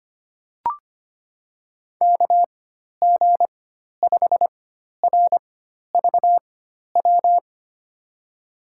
Morse code, heard as a keyed tone, about 25 wpm, spaced as for 12 wpm, 700 Hz. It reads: KG5RVW
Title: USA Call Signs - 6 Characters - 4x Character Spacing - 25wpm